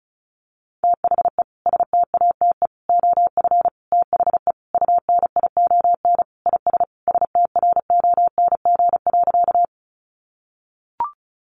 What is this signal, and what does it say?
Morse code, heard as a keyed tone, about 34 wpm, 700 Hz. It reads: THE STATE OF THE UNION IS STRONG.
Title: Sentences from Top 1400 Words 35wpm